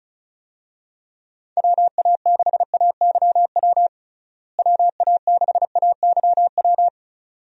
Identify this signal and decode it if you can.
Morse code, heard as a keyed tone, about 34 wpm, 700 Hz. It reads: WA6AYW WA6AYW
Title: USA Call Signs - 6 Characters - Repeated 2x - 35wpm